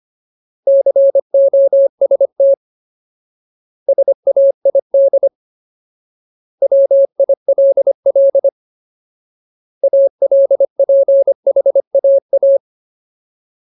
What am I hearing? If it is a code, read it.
Morse code, heard as a keyed tone, about 25 wpm, 550 Hz. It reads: COST SAID WILL ALPHAA
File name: Ultimate Word List / 25WPM Ultimate